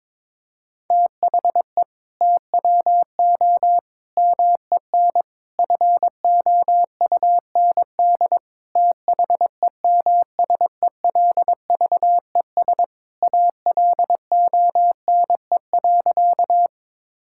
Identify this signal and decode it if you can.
Morse code, heard as a keyed tone, about 22 wpm, 700 Hz. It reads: THE TWO MEN FOUND THEMSELVES ALONE.
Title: Sentences from Top 1000 Words - Review Part 5 - 22wpm